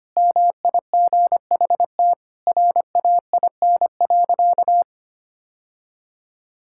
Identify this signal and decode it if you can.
Morse code, heard as a keyed tone, about 25 wpm, 700 Hz. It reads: MIGHT RAIN.
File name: practice / 25wpm / Sentences with 3-Word Phrases - 25wpm